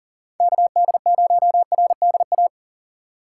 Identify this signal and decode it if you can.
Morse code, heard as a keyed tone, about 40 wpm, 700 Hz. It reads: KD0RDA
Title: USA Call Signs - 4x Character Spacing - 40wpm